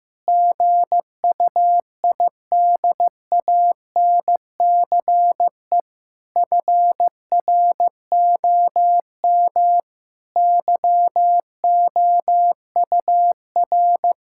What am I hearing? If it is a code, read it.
Morse code, heard as a keyed tone, about 16 wpm, 700 Hz. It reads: GUIDANCE FROM YOUR
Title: Sentences from Top 1600 Words 15wpm